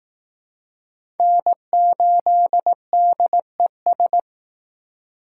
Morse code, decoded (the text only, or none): N8DES